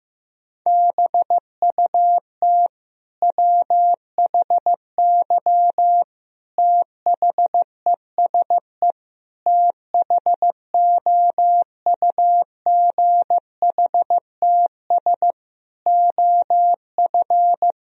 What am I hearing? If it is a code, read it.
Morse code, heard as a keyed tone, about 15 wpm, 700 Hz. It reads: BUT WHY THESE THOUGHTS OF
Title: Sentences from Top 1000 Words - Review Part 1 - 15wpm